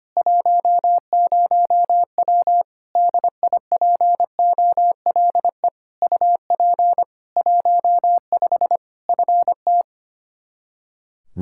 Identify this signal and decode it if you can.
Morse code, heard as a keyed tone, about 25 wpm, 700 Hz. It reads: 10W DIPOLE UP 15 FT
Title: Practice QSO - Exchange 2 - Info - 25wpm